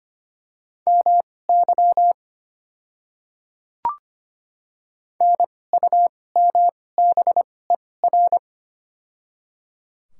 Morse code, heard as a keyed tone, about 25 wpm, spaced as for 18 wpm, 700 Hz. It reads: MY NUMBER
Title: 2x Character Spacing - Top 100 Words - 25wpm